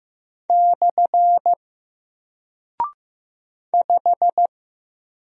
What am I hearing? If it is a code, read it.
Morse code, heard as a keyed tone, about 15 wpm, 700 Hz. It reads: / 5
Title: Single Character (Letter:Number:Punctuation:Prosign) Alt 15wpm